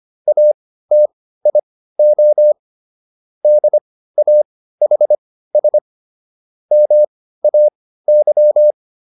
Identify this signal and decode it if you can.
Morse code, heard as a keyed tone, about 25 wpm, spaced as for 15 wpm, 600 Hz. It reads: ATIO DAHS MAY